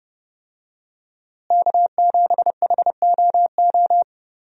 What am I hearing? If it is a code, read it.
Morse code, heard as a keyed tone, about 32 wpm, 700 Hz. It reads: K7HOO
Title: USA Call Signs - 5 Characters - Repeated 3x - 30wpm